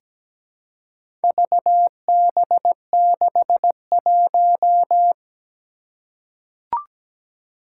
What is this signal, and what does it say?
Morse code, heard as a keyed tone, about 17 wpm, 700 Hz. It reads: VB61
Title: Four Challenging Letter-Number Combinations 17wpm